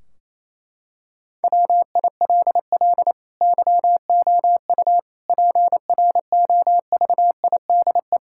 Morse code, heard as a keyed tone, about 28 wpm, 700 Hz. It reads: WILL YOU PROVIDE
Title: Sentences from Top 1600 Words 28wpm